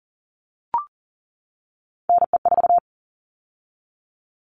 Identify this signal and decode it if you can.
Morse code, heard as a keyed tone, about 40 wpm, 700 Hz. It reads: NE4
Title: All Cut Numbers - 3-Digit 40wpm